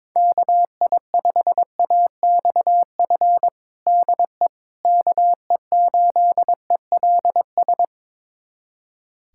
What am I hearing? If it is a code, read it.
Morse code, heard as a keyed tone, about 22 wpm, 700 Hz. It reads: KI5AXF DE KE8ELS
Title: USA Callsign DE Callsign - 6 Characters - 22wpm